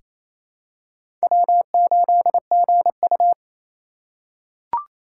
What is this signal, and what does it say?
Morse code, heard as a keyed tone, about 28 wpm, 700 Hz. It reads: W8GU
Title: K1USN SST QSO Practice - 28wpm